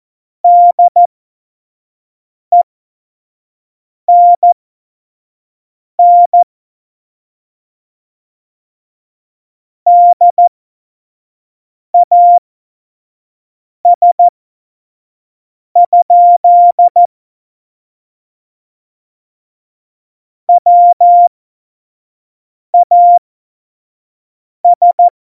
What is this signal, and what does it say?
Morse code, heard as a keyed tone, about 14 wpm, spaced as for 5 wpm, 700 Hz. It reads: DENN DAS? WAS
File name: Morsekurs / Witze / Witz 13. Februar 2018, 25 F70